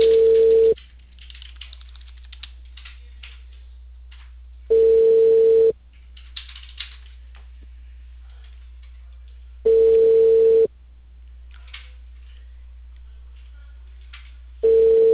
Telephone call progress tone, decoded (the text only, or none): ringback tone